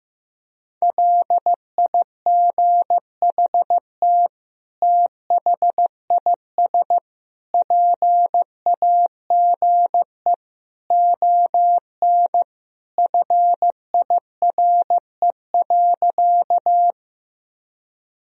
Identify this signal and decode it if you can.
Morse code, heard as a keyed tone, about 16 wpm, 700 Hz. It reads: LIGHT THIS PAGE ON FIRE.
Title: Sentences from Top 1000 Words - Review Part 1 - 15wpm